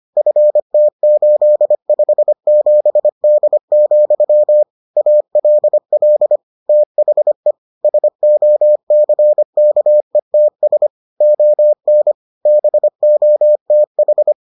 Morse code, heard as a keyed tone, about 25 wpm, 600 Hz. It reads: FT857D, ALL THE SOCKETS ON BOTH